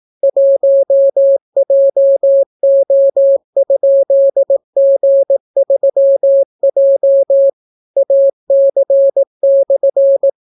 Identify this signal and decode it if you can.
Morse code, heard as a keyed tone, about 18 wpm, 550 Hz. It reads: 1JO?G3J AC/